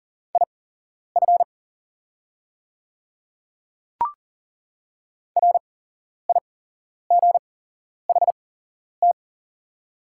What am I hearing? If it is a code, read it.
Morse code, heard as a keyed tone, about 40 wpm, spaced as for 11 wpm, 690 Hz. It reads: IF RIGHT
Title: 8x Character Spacing - Top 200 Words - 40wpm